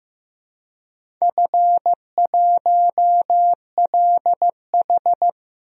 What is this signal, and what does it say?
Morse code, heard as a keyed tone, about 15 wpm, 700 Hz. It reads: F1LH
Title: DXCC-100 Call Signs - 4 Characters - 15wpm